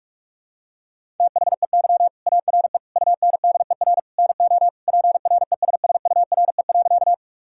Morse code, heard as a keyed tone, about 42 wpm, 700 Hz. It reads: THEY ARE UNDER NO PRESSURE.
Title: Sentences from Top 1000 Words - Review Part 6 - 45wpm